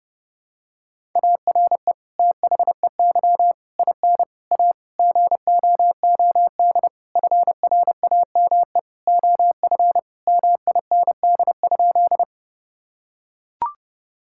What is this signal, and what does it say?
Morse code, heard as a keyed tone, about 30 wpm, 700 Hz. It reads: ARE THEY IN A GOOD FRAME OF MIND?
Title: Sentences from Top 1000 Words - Review Part 6 - 30wpm